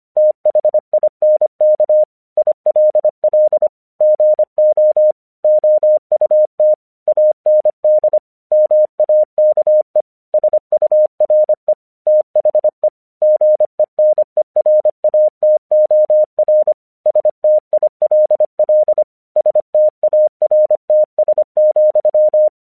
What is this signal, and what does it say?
Morse code, heard as a keyed tone, about 25 wpm, 610 Hz. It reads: THINK ILL GO OUT AND MAKE SURE THE GENERATOR STILL STARTS,
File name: 1 English 25wpm / AQC16